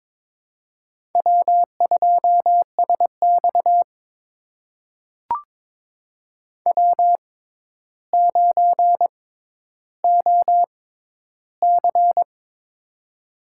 Morse code, heard as a keyed tone, about 22 wpm, spaced as for 11 wpm, 700 Hz. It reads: W2SX W9OC
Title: USA Call Signs - 4 Characters - 6x Character Spacing - 22wpm